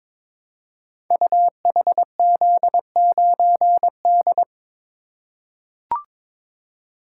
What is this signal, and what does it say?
Morse code, heard as a keyed tone, about 22 wpm, 700 Hz. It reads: UHZ9D
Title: Code Groups - 5 Letter:Numbers - 22wpm